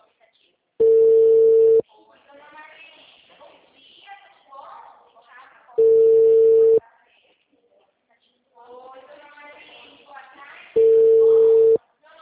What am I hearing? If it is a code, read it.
Ringback tone (China, 450 Hz), with speech.